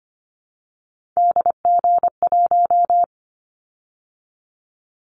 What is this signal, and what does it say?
Morse code, heard as a keyed tone, about 25 wpm, 700 Hz. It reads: DG1
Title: Callsign Prefixes for Top 25 DXCC - 25wpm